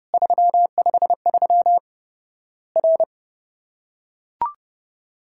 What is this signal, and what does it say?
Morse code, heard as a keyed tone, about 30 wpm, 700 Hz. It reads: 353 R
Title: Worked All Europe DX Contest - QTC Exchange - 30wpm